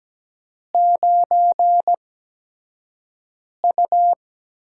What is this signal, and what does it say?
Morse code, heard as a keyed tone, about 17 wpm, spaced as for 5 wpm, 700 Hz. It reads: 9U
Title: USA Call Signs - 4 Characters - 8x Character Spacing - 17wpm